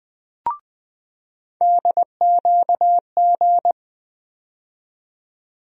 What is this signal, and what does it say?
Morse code, heard as a keyed tone, about 20 wpm, 700 Hz. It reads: DQG
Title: USA Call Sign Postfixes - 3 Letters 20wpm